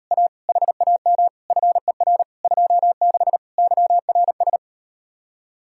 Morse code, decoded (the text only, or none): A HAM FER 26 YRS